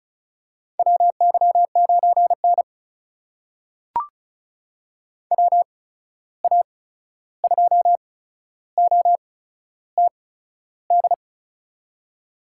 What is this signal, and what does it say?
Morse code, heard as a keyed tone, about 36 wpm, spaced as for 13 wpm, 700 Hz. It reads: WY9N WA2OTD
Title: USA Call Signs - 8x Character Spacing - 35wpm